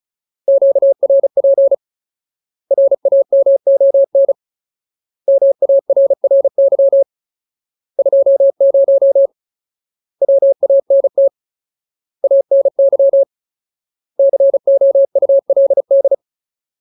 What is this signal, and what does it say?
Morse code, heard as a keyed tone, about 35 wpm, 550 Hz. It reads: QRP RAMON MARRY 20 WANT ANY COULD
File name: Ultimate Word List / 35WPM Ultimate